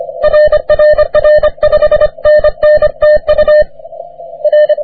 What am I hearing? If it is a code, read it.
Morse code, heard as a keyed tone, about 26 wpm, 600 Hz. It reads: RRR 5NN TU R
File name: wp / k4m 40m cw 17 oct 2009 0729z